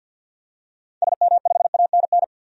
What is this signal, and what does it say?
Morse code, heard as a keyed tone, about 51 wpm, 700 Hz. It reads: SM5ANN
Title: DXCC-100 Call Signs - 6 Characters - 50wpm